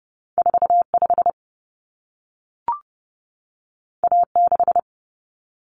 Morse code, heard as a keyed tone, about 30 wpm, 700 Hz. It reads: 45 A6
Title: Common Cut Numbers - 2-Digit 30wpm